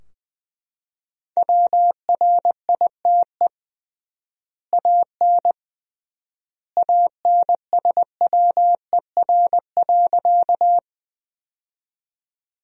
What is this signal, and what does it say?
Morse code, heard as a keyed tone, about 20 wpm, 700 Hz. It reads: WRITE AN ANSWER.
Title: 3x Word-Spacing - Sentences from Top 300 Words - 20wpm